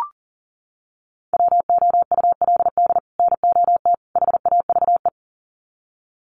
Morse code, heard as a keyed tone, about 40 wpm, 700 Hz. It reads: WOULD NOT HAVE
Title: Sets of 3 Words 40wpm